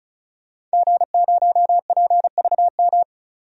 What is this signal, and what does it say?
Morse code, heard as a keyed tone, about 36 wpm, 700 Hz. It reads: G0PVM
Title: DXCC-100 Call Signs - 5 Characters - 35wpm